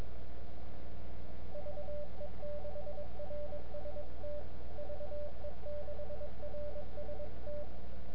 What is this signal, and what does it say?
Morse code, heard as a keyed tone, about 21 wpm, 590 Hz, with noise. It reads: VE6RST VE6RST
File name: n6tr / WS7L EI2CN